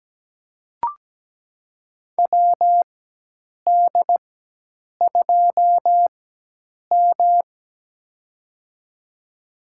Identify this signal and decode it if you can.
Morse code, heard as a keyed tone, about 17 wpm, spaced as for 8 wpm, 700 Hz. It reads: WD2M
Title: USA Call Signs - 4 Characters - 4x Character Spacing - 17wpm